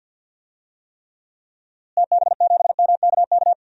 Morse code, heard as a keyed tone, about 52 wpm, 700 Hz. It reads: TB7MKK
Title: DXCC-100 Call Signs - 6 Characters - Repeated 2x - 50wpm